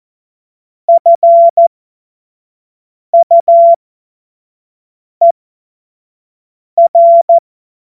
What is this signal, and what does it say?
Morse code, heard as a keyed tone, about 14 wpm, spaced as for 5 wpm, 680 Hz. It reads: FUER
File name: Morsekurs / Witze / Witz 15. Juni 2021, 25 F70